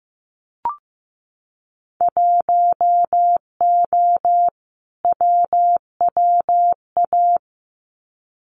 Morse code, heard as a keyed tone, about 15 wpm, 700 Hz. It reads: 1O WWA
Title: Winter Field Day Exchanges 15wpm